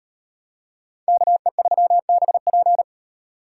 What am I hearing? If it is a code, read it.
Morse code, heard as a keyed tone, about 38 wpm, 700 Hz. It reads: KE3BP